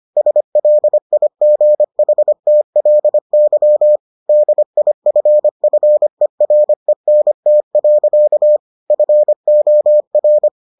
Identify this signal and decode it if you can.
Morse code, heard as a keyed tone, about 25 wpm, 600 Hz. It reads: SLIGHTLY DIFFERENT. FOR